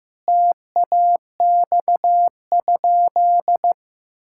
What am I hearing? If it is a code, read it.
Morse code, heard as a keyed tone, about 15 wpm, 700 Hz. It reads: TAX?
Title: Sentences from Top 1000 Words - Review Part 5 - 15wpm